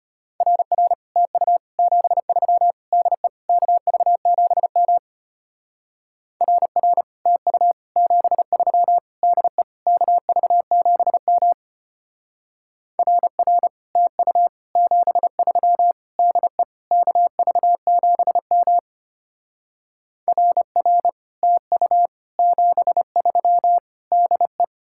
Morse code, heard as a keyed tone, about 32 wpm, 700 Hz. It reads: RR TU 73 DE KV7M RR TU 73 DE KV7M RR TU 73 DE KV7M RR TU 73 DE